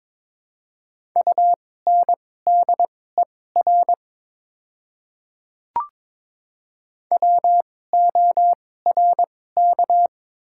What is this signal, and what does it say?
Morse code, heard as a keyed tone, about 22 wpm, spaced as for 16 wpm, 700 Hz. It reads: UNDER WORK
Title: 2x Character Spacing - Top 200 Words - 22wpm